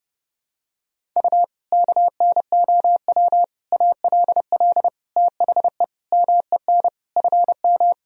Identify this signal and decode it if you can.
Morse code, heard as a keyed tone, about 30 wpm, 700 Hz. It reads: U KNOW ALL THE MEN FM
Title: Sentences from Top 300 Words and Abbreviations 30wpm